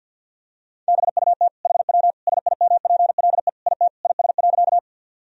Morse code, heard as a keyed tone, about 50 wpm, 700 Hz. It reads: BUT HW SIMPLE IT IS.
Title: Sentences from Top 400 Words and Abbreviations 50wpm